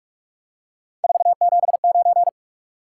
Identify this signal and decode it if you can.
Morse code, heard as a keyed tone, about 46 wpm, 700 Hz. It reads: 479
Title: Top 180 QSO Elements 45wpm